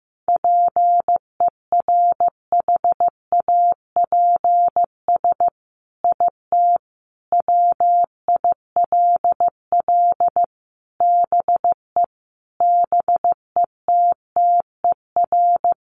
Morse code, heard as a keyed tone, about 15 wpm, 700 Hz. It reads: PERHAPS IT WILL BE BETTER
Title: Sentences from Top 1500 Words 15wpm